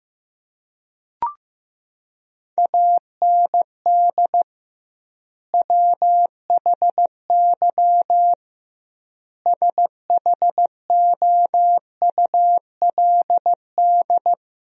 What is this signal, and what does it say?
Morse code, heard as a keyed tone, about 15 wpm, 700 Hz. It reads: AND WHY SHOULD
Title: 2x Word-Spacing - Sentences from Top 300 Words - 15wpm